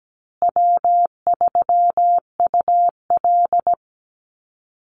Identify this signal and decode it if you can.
Morse code, heard as a keyed tone, about 17 wpm, 700 Hz. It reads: W3UL
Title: Easy Call Signs Repeated 3x - 17wpm